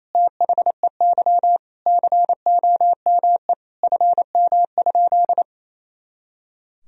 Morse code, heard as a keyed tone, about 28 wpm, 700 Hz. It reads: THEY COME FM?
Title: Sentences from Top 200 Words and Abbreviations 28wpm